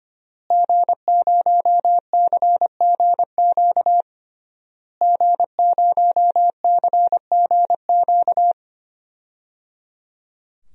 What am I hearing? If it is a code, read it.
Morse code, heard as a keyed tone, about 25 wpm, 700 Hz. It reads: G0CGQ G0CGQ
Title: DXCC-100 Call Signs - 5 Characters - Repeated 3x - 25wpm